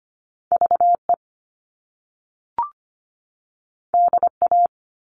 Morse code, heard as a keyed tone, about 25 wpm, 700 Hz. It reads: VE DA
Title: All Cut Numbers - 2-Digit 25wpm